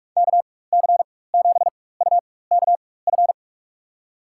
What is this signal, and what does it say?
Morse code, heard as a keyed tone, about 46 wpm, spaced as for 21 wpm, 700 Hz. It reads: KC7UKF
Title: USA Call Signs - 4x Character Spacing - 45wpm